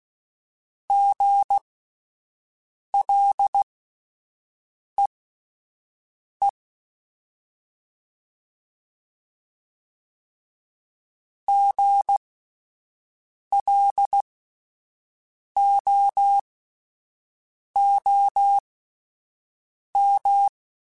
Morse code, heard as a keyed tone, about 16 wpm, spaced as for 5 wpm, 800 Hz. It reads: GLEE GLOOM
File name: Spaced Alphas / Spaced Alphas mp3s / Gs sp @10